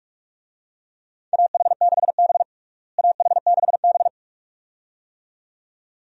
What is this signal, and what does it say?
Morse code, heard as a keyed tone, about 46 wpm, 700 Hz. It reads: AH6B AH6B
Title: USA Call Signs - 4 Characters - Repeated 2x - 45wpm